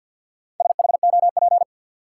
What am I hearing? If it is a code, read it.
Morse code, heard as a keyed tone, about 52 wpm, 700 Hz. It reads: SHOP
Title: Sentences from Top 1600 Words - New Word Drill Alt - 50wpm